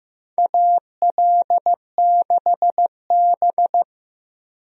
Morse code, heard as a keyed tone, about 15 wpm, 700 Hz. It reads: AL6B
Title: USA Call Signs - 4 Characters - Repeated 3x - 15wpm